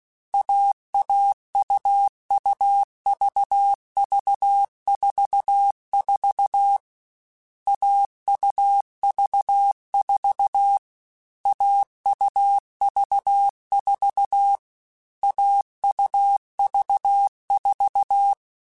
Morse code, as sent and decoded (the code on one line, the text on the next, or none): .- .- ..- ..- ...- ...- ....- ....- / .- ..- ...- ....- / .- ..- ...- ....- / .- ..- ...- ....-
AAUUVV44 AUV4 AUV4 AUV4